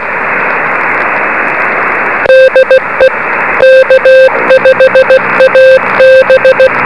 Morse code, heard as a keyed tone, about 16 wpm, 520 Hz, with noise. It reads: DE K5AB
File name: beacons / k5ab